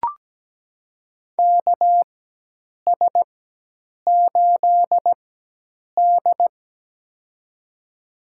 Morse code, heard as a keyed tone, about 17 wpm, spaced as for 8 wpm, 700 Hz. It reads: KS8D
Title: USA Call Signs - 4 Characters - 4x Character Spacing - 17wpm